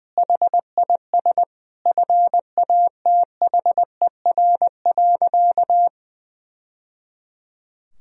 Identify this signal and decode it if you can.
Morse code, heard as a keyed tone, about 20 wpm, 700 Hz. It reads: HIS FATHER.
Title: Sentences from Top 1000 Words - Review Part 2 - 20wpm